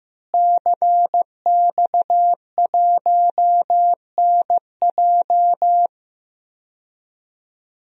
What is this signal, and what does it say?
Morse code, heard as a keyed tone, about 16 wpm, 700 Hz. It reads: CX1NJ
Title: DXCC-100 Call Signs - 5 Characters - Repeated 2x - 15wpm